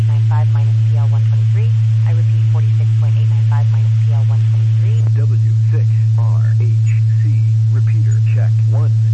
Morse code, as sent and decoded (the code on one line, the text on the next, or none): none